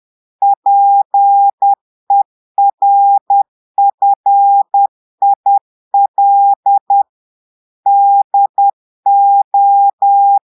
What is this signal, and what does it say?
Morse code, heard as a keyed tone, about 10 wpm, 800 Hz. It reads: PERFIL DO
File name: portugais texte en clair 2 vitesse 10 mots minutes